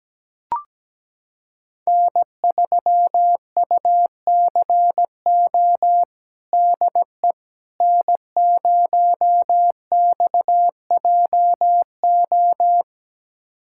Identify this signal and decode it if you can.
Morse code, heard as a keyed tone, about 17 wpm, 700 Hz. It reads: N3UCO DE N0XJO
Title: USA Callsign DE Callsign - 5 Characters - 17wpm